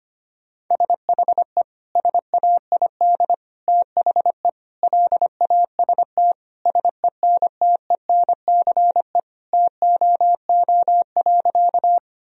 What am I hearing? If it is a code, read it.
Morse code, heard as a keyed tone, about 25 wpm, 700 Hz. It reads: SHE SAID THE LAST SENTENCE TOO.
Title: Sentences from Top 1000 Words - Review Part 1 - 25wpm